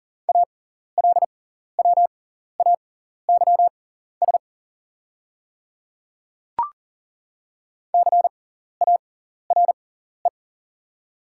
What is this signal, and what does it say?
Morse code, heard as a keyed tone, about 41 wpm, spaced as for 14 wpm, 690 Hz. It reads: ALWAYS CARE
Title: 6x Character Spacing - Top 300 Words - 40wpm